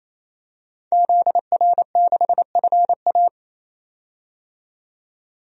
Morse code, heard as a keyed tone, about 28 wpm, 700 Hz. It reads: ZR6FA
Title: DXCC-100 Call Signs - 5 Characters - 28wpm